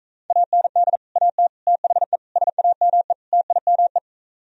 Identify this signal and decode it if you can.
Morse code, heard as a keyed tone, about 45 wpm, 700 Hz. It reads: AND AT THE SAME TIME